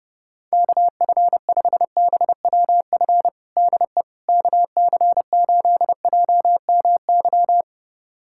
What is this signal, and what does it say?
Morse code, heard as a keyed tone, about 32 wpm, 700 Hz. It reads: KF5BWF DE KC8JMY
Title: USA Callsign DE Callsign - 6 Characters - 30wpm